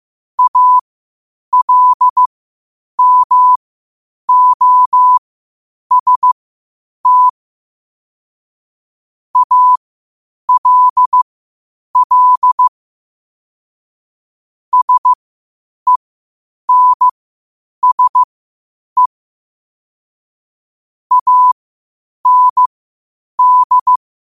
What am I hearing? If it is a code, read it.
Morse code, heard as a keyed tone, about 15 wpm, spaced as for 8 wpm, 1000 Hz. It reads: ALMOST ALL SENSE AND
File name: qotdfiles / 8wpm